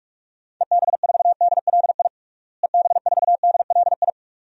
Morse code, heard as a keyed tone, about 41 wpm, 690 Hz. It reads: EB4DLI EB4DLI
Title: DXCC-100 Call Signs - 6 Characters - Repeated 2x - 45wpm